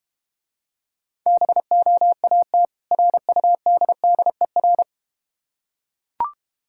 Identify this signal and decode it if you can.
Morse code, heard as a keyed tone, about 31 wpm, 700 Hz. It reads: BOAT RUDDER
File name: practice / 25wpm / Difficult Words - Speed-Racing with Hints - 25wpm